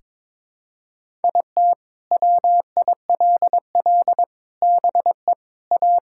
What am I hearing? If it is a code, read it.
Morse code, heard as a keyed tone, about 22 wpm, 700 Hz. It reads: IT WILL BE A
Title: Sentences from Top 1600 Words 22wpm